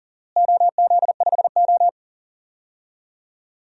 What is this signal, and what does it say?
Morse code, heard as a keyed tone, about 40 wpm, 700 Hz. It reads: OZ5O
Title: DXCC-100 Call Signs - 4 Characters - Repeated 3x - 40wpm